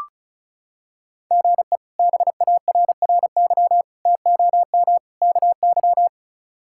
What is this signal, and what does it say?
Morse code, heard as a keyed tone, about 34 wpm, 700 Hz. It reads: GE BARRY TOM KY
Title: K1USN SST QSO Practice - 35wpm